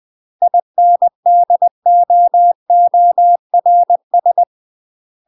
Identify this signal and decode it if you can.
Morse code, heard as a keyed tone, about 20 wpm, 700 Hz. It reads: INDOORS